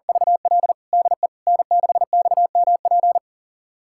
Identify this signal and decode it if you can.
Morse code, heard as a keyed tone, about 40 wpm, 700 Hz. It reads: VL DE N6XMP
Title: USA Callsign DE Callsign - 5 Characters - 40wpm